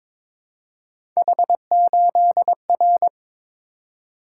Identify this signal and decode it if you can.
Morse code, heard as a keyed tone, about 23 wpm, 700 Hz. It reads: H8R
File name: practice / 22wpm / Three Letter-Number Combinations - 22wpm